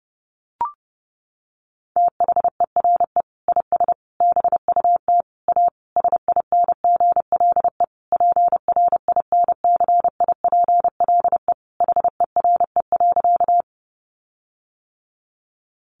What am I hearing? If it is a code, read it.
Morse code, heard as a keyed tone, about 30 wpm, 700 Hz. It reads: THERE IS BUT A SINGLE PRINCIPLE HERE.
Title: Sentences from Top 1400 Words 30wpm